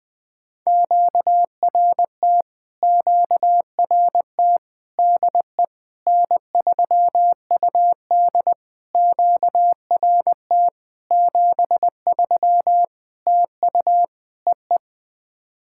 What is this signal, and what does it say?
Morse code, heard as a keyed tone, about 20 wpm, 700 Hz. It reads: QRT QRT DE N3UD QRT 73 TU EE
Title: POTA QSO Practice - 20wpm